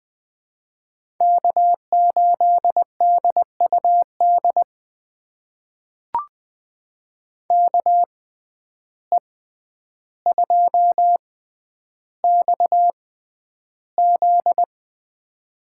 Morse code, heard as a keyed tone, about 20 wpm, spaced as for 10 wpm, 700 Hz. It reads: K8DUD KE2XZ
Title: USA Call Signs - 5 Characters - 6x Character Spacing - 20wpm